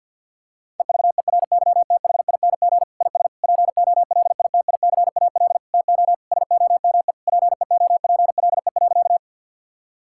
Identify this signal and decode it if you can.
Morse code, heard as a keyed tone, about 50 wpm, 700 Hz. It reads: EVERYTHING IS POLITICAL TO SOME PEOPLE.
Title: Sentences from Top 1000 Words - Review Part 5 - 50wpm